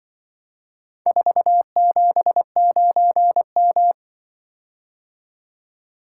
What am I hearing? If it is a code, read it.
Morse code, heard as a keyed tone, about 25 wpm, 700 Hz. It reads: 479M